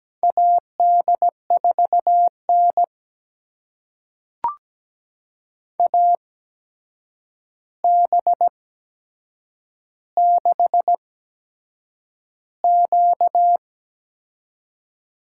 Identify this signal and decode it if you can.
Morse code, heard as a keyed tone, about 17 wpm, spaced as for 7 wpm, 700 Hz. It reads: AD4N AB6Q
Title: USA Call Signs - 4 Characters - 8x Character Spacing - 17wpm